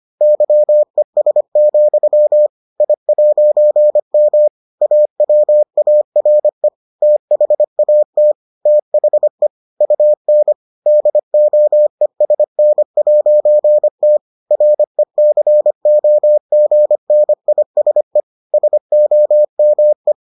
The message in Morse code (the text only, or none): YES, I'M AWARE THAT THE UN DOESN'T RECOGNISE SOME